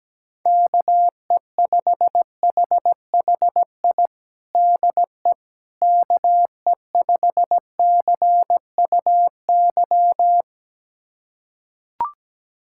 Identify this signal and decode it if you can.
Morse code, heard as a keyed tone, about 17 wpm, 700 Hz. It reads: KE5HHI DE KE5CUY
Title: USA Callsign DE Callsign - 6 Characters - 17wpm